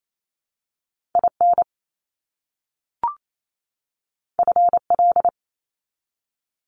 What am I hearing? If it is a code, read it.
Morse code, heard as a keyed tone, about 28 wpm, 700 Hz. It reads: IN FL
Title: US States 28wpm